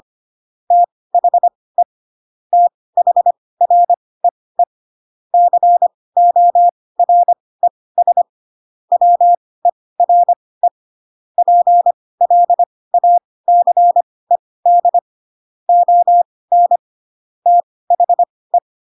Morse code, heard as a keyed tone, about 25 wpm, spaced as for 18 wpm, 700 Hz. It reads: THE THREE CORES WERE PLACED ON THE
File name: Pinocchio / Pinocchio ch7 12 25 ik1hge